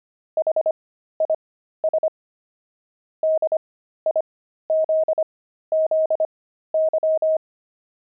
Morse code, heard as a keyed tone, about 25 wpm, spaced as for 13 wpm, 630 Hz. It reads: HIS DIZZY
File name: pangram-fs-13-cs-25-gloved